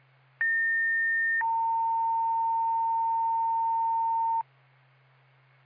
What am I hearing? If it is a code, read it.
Two-tone page, heard as a one-second tone, then a three-second tone, about 1800 then 910 Hz. It is heard from 0.4 to 4.4 s.